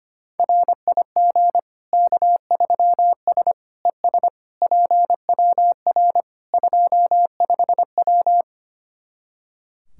Morse code, heard as a keyed tone, about 25 wpm, 700 Hz. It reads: RIG K3S ES PWR 25W
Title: Practice QSO - Exchange 2 - Info - 25wpm